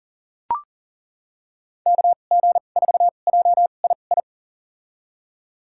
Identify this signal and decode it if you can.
Morse code, heard as a keyed tone, about 38 wpm, spaced as for 28 wpm, 700 Hz. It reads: KG4JII
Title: USA Call Signs - 6 Characters - 2x Character Spacing - 40wpm